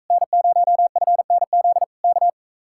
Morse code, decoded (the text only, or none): N0FNZ K